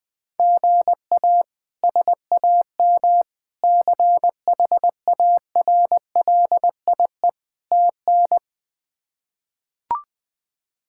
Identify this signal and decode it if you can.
Morse code, heard as a keyed tone, about 21 wpm, 700 Hz. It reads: GA SAM CHARLIE TN
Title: K1USN SST QSO Practice - 20wpm